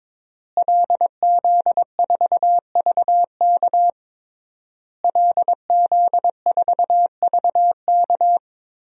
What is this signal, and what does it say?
Morse code, heard as a keyed tone, about 22 wpm, 700 Hz. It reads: LZ4VK LZ4VK
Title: DXCC-100 Call Signs - 5 Characters - Repeated 3x - 22wpm